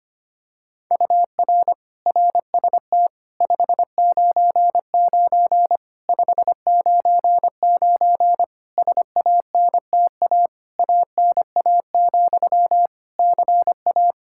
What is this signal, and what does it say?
Morse code, heard as a keyed tone, about 25 wpm, 700 Hz. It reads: UR RST 599 599 SANTA ANA, CA
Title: Practice QSO - Exchange 1 - Beginning - 25wpm